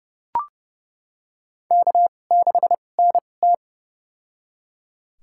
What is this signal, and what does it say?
Morse code, heard as a keyed tone, about 29 wpm, spaced as for 21 wpm, 700 Hz. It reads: K6NT
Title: USA Call Signs - 4 Characters - 2x Character Spacing - 30wpm